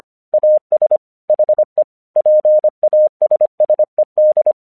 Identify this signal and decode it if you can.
Morse code, heard as a keyed tone, about 25 wpm, 610 Hz. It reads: AS HE PASSED
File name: 1 English 25wpm / AQC16